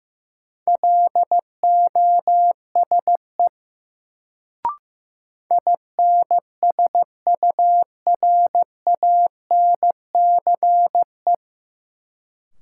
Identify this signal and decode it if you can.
Morse code, heard as a keyed tone, about 15 wpm, 700 Hz. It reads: LOSE INSURANCE
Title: Sentences from Top 1100 Words - New Word Drill - 15wpm